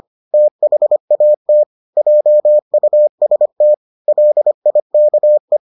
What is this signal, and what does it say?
Morse code, heard as a keyed tone, about 25 wpm, 600 Hz. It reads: THAT JUST LIKE